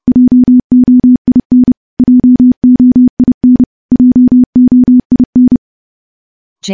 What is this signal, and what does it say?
Morse code, heard as a keyed tone, about 30 wpm, 250 Hz. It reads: JOIN JOIN JOIN
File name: morse / morse-training-30WPM